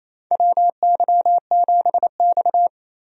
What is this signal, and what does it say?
Morse code, heard as a keyed tone, about 28 wpm, 700 Hz. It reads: WY7X
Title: USA Call Signs - 4 Characters - 28wpm